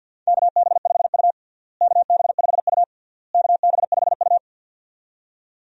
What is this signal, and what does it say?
Morse code, heard as a keyed tone, about 52 wpm, 700 Hz. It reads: KB5U KB5U KB5U
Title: USA Call Signs - 4 Characters - Repeated 3x - 50wpm